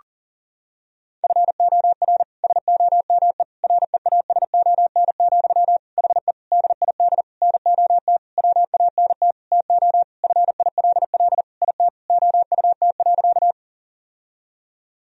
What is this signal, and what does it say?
Morse code, heard as a keyed tone, about 40 wpm, 700 Hz. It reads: FOR SOME REASON, HE DID NOT WANT TO FILL IT OUT.
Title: Sentences from Top 1000 Words - Review Part 3 - 40wpm